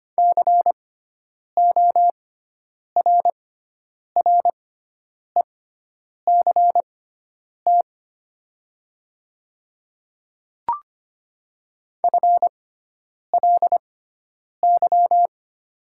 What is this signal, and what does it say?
Morse code, heard as a keyed tone, about 25 wpm, spaced as for 9 wpm, 700 Hz. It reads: CORRECT FLY
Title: 6x Character Spacing - Top 500 Words - 25wpm